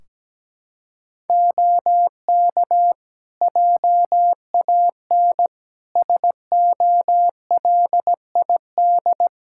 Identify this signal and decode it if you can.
Morse code, heard as a keyed tone, about 17 wpm, 700 Hz. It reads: OK JAN SOLID
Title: K1USN SST QSO Practice 17wpm